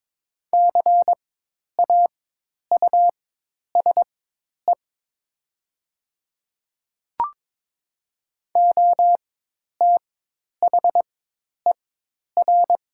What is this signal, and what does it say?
Morse code, heard as a keyed tone, about 22 wpm, spaced as for 10 wpm, 700 Hz. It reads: CAUSE OTHER
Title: 4x Character Spacing - Top 200 Words - 22wpm